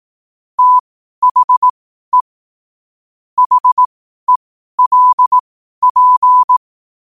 Morse code, heard as a keyed tone, about 18 wpm, spaced as for 12 wpm, 1000 Hz. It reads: THE HELP
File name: qotdfiles / 12wpm